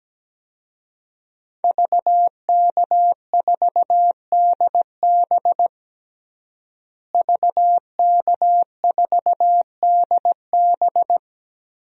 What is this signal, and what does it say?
Morse code, heard as a keyed tone, about 17 wpm, 700 Hz. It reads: VK4DB VK4DB
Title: DXCC-100 Call Signs - 5 Characters - Repeated 3x - 17wpm